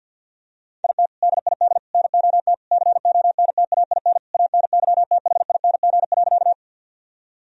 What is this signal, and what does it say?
Morse code, heard as a keyed tone, about 50 wpm, 700 Hz. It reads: IT DID NOT CONTAIN ANYTHING.
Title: Sentences from Top 1000 Words - Review Part 2 - 50wpm